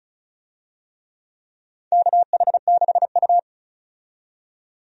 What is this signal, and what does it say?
Morse code, heard as a keyed tone, about 36 wpm, 700 Hz. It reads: KH6U
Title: DXCC-100 Call Signs - 4 Characters - 35wpm